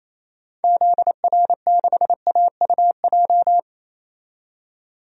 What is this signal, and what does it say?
Morse code, heard as a keyed tone, about 28 wpm, 700 Hz. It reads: ZR6AUJ
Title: DXCC-100 Call Signs - 6 Characters - 28wpm